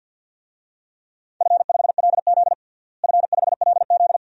Morse code, heard as a keyed tone, about 51 wpm, 700 Hz. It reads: F5LZ F5LZ
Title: DXCC-100 Call Signs - 4 Characters - Repeated 2x - 50wpm